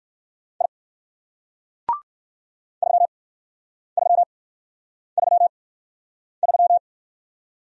Morse code, heard as a keyed tone, about 55 wpm, spaced as for 9 wpm, 700 Hz. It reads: I 3333
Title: Speed-Racing Single Letter-Number 45wpm